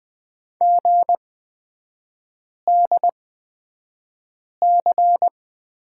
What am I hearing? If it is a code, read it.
Morse code, heard as a keyed tone, about 20 wpm, spaced as for 5 wpm, 700 Hz. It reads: GDC
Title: 250 Lesson 40e - BK - Three Characters 20wpm